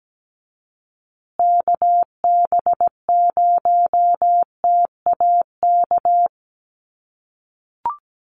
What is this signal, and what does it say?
Morse code, heard as a keyed tone, about 17 wpm, 700 Hz. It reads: KB0TAK
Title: USA Call Signs Repeated 3x 17wpm